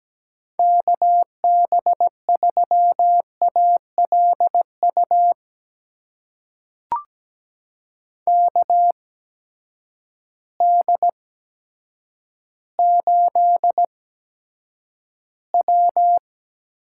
Morse code, heard as a keyed tone, about 17 wpm, spaced as for 8 wpm, 700 Hz. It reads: KB3ALU KD8W
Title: USA Call Signs - 6 Characters - 8x Character Spacing - 17wpm